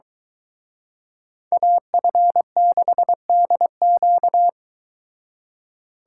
Morse code, heard as a keyed tone, about 22 wpm, 700 Hz. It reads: AF6DQ